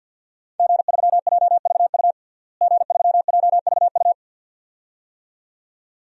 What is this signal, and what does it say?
Morse code, heard as a keyed tone, about 47 wpm, 700 Hz. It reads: G3JVU G3JVU
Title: DXCC-100 Call Signs - 5 Characters - Repeated 2x - 50wpm